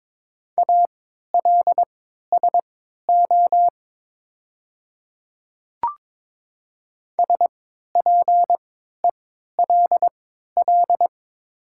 Morse code, heard as a keyed tone, about 22 wpm, spaced as for 13 wpm, 690 Hz. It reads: ALSO SPELL
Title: 3x Character Spacing - Top 200 Words - 22wpm